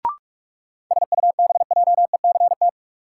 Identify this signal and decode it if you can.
Morse code, heard as a keyed tone, about 45 wpm, 700 Hz. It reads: SUBJECT
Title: Sentences from Top 800 Words - New Word Drill - 45wpm